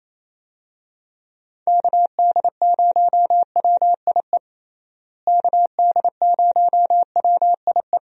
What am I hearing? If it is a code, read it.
Morse code, heard as a keyed tone, about 28 wpm, 700 Hz. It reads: KD0WIE KD0WIE
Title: USA Call Signs - 6 Characters - Repeated 2x - 28wpm